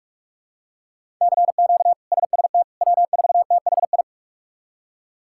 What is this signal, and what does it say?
Morse code, heard as a keyed tone, about 46 wpm, 700 Hz. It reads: CQ SST W4THI
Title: K1USN SST QSO Practice - 45wpm